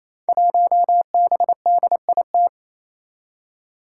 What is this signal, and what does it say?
Morse code, heard as a keyed tone, about 27 wpm, 700 Hz. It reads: 1BDIT
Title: Five Challenging Letter-Number Combinations 28wpm